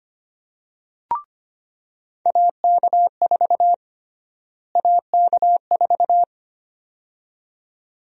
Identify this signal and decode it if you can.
Morse code, heard as a keyed tone, about 25 wpm, 700 Hz. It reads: AK4 AK4
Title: USA Call Sign Prefixes - Repeated 2x - 25wpm